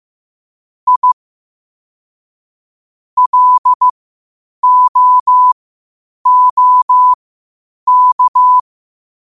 Morse code, heard as a keyed tone, about 15 wpm, spaced as for 8 wpm, 1000 Hz. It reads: I LOOK
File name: qotdfiles / 8wpm